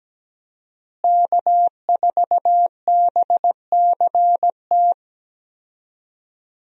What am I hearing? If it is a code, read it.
Morse code, heard as a keyed tone, about 17 wpm, 700 Hz. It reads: K4BCT
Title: USA Call Signs - 5 Characters - Repeated 3x - 17wpm